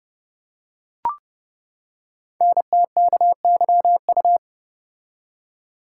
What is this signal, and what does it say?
Morse code, heard as a keyed tone, about 30 wpm, 700 Hz. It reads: NTKYU
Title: Code Groups - 5 Letters - 30wpm